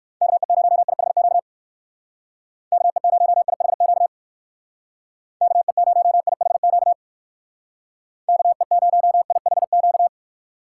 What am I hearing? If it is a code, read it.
Morse code, heard as a keyed tone, about 59 wpm, 700 Hz. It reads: KE0IHQ KE0IHQ KE0IHQ KE0IHQ